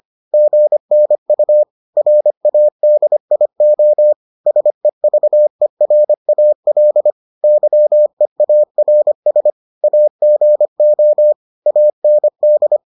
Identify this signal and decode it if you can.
Morse code, heard as a keyed tone, about 25 wpm, 600 Hz. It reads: GNU RADIO SEVERAL YEARS AGO AND